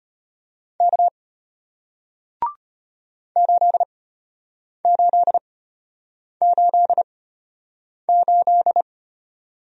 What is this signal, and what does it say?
Morse code, heard as a keyed tone, about 33 wpm, spaced as for 8 wpm, 700 Hz. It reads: K 8888